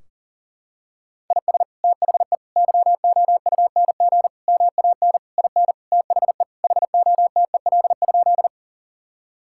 Morse code, heard as a keyed tone, about 40 wpm, 700 Hz. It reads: IS THE YOUNG MAN IN THE HOTEL?
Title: Sentences from Top 1600 Words 40wpm